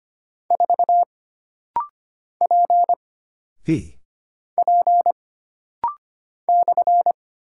Morse code, heard as a keyed tone, about 25 wpm, 700 Hz. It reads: 4 P P /
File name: Single Character (Letter:Number:Punctuation:Prosign) - Rapid-Fire - 25wpm